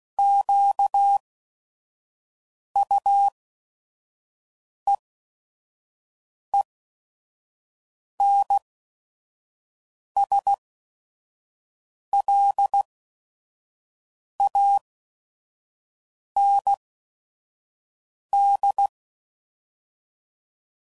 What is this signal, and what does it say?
Morse code, heard as a keyed tone, about 16 wpm, spaced as for 5 wpm, 800 Hz. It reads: QUEENSLAND
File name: CW Code Groups @5WPM / fromSM's / Cities